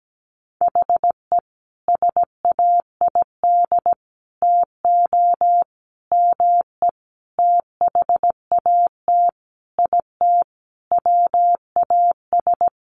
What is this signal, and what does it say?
Morse code, heard as a keyed tone, about 17 wpm, 700 Hz. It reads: HE SAID TO ME THAT IT WAS
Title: Sentences from Top 1500 Words 17wpm